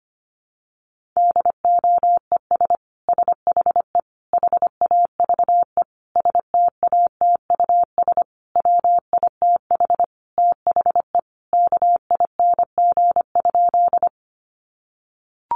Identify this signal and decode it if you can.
Morse code, heard as a keyed tone, about 25 wpm, 700 Hz. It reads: DOES SHE HAVE STATUS WITH THE KING?
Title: Sentences from Top 1400 Words 25wpm